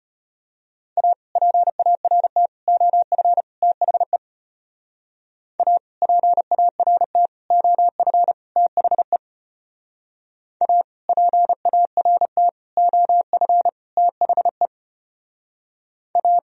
Morse code, heard as a keyed tone, about 33 wpm, 700 Hz. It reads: A PART OF THE A PART OF THE A PART OF THE A